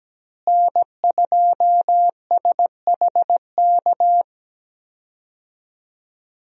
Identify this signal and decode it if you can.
Morse code, heard as a keyed tone, about 17 wpm, 700 Hz. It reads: N2SHK
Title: USA Call Signs - 5 Characters - Repeated 3x - 17wpm